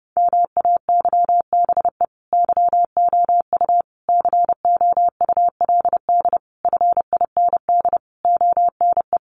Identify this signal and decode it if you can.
Morse code, heard as a keyed tone, about 30 wpm, 700 Hz. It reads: MAYBE YOU COULD FIND ONE
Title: Sentences from Top 1400 Words 30wpm